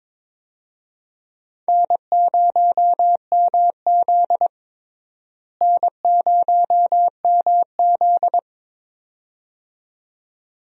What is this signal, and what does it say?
Morse code, heard as a keyed tone, about 22 wpm, 700 Hz. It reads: N0MZ N0MZ
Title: USA Call Signs - 4 Characters - Repeated 2x - 22wpm